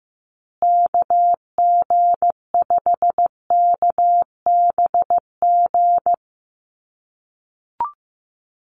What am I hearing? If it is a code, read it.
Morse code, heard as a keyed tone, about 15 wpm, 700 Hz. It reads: KG5KBG
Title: USA Call Signs Repeated 3x 15wpm